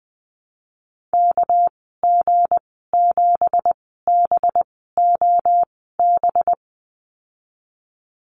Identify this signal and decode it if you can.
Morse code, heard as a keyed tone, about 20 wpm, spaced as for 15 wpm, 700 Hz. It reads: KG7BOB
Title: 207 Lesson 33g - Letter j - Call Signs 20wpm